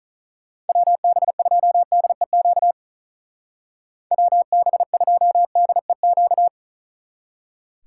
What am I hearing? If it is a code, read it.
Morse code, heard as a keyed tone, about 43 wpm, 700 Hz. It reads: WB2DEQ WB2DEQ
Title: Speed Racing - USA 2x3 Call Signs 35wpm